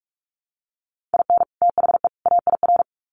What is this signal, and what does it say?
Morse code, heard as a keyed tone, about 44 wpm, 700 Hz. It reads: IN THE AIR
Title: Sets of 3 Words 45wpm